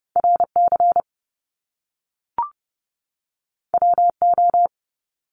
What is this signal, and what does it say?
Morse code, heard as a keyed tone, about 30 wpm, 700 Hz. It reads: RC WO
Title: Top Two Letter Combinations in Words 30wpm